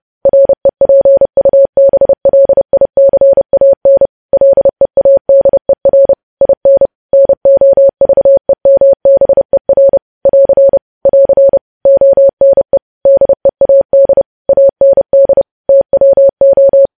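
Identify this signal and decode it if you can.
Morse code, heard as a keyed tone, about 30 wpm, 550 Hz. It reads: REPUBLICAN LEADER IN NOVEMBER <AR> <AR> ONE DEAD AND TWO
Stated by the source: News headlines sent in Morse at 30 wpm, preceded by VVV.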